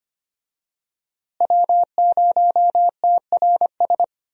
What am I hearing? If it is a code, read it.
Morse code, heard as a keyed tone, about 25 wpm, 700 Hz. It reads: W0TRS
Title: USA Call Signs - 5 Characters - Repeated 2x - 25wpm